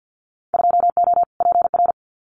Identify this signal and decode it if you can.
Morse code, heard as a keyed tone, about 50 wpm, 700 Hz. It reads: 2O PR
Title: Winter Field Day Exchanges Encore 50wpm